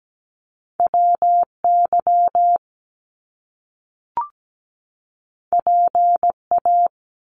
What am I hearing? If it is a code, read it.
Morse code, heard as a keyed tone, about 17 wpm, 700 Hz. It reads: WY PA
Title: US States 17wpm